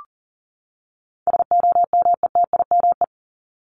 Morse code, heard as a keyed tone, about 40 wpm, 700 Hz. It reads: SOMETIME
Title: ES:S Suffix - Top 100 Words with Base Words 40wpm